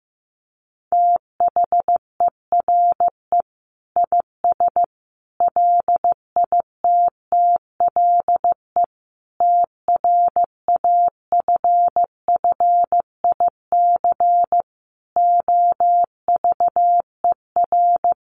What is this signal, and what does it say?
Morse code, heard as a keyed tone, about 15 wpm, 700 Hz. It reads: THERE IS LITTLE TRAFFIC OVER
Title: Sentences from Top 1300 Words 15wpm